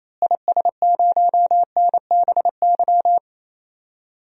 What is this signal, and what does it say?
Morse code, heard as a keyed tone, about 28 wpm, 700 Hz. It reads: IS0NBY
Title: DXCC-100 Call Signs - 6 Characters - Repeated 3x - 28wpm